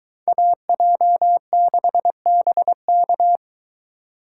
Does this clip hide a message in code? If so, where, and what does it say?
Yes: Morse code from 0.3 to 3.4 s, heard as a keyed tone, about 22 wpm, 700 Hz. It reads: AJ6BK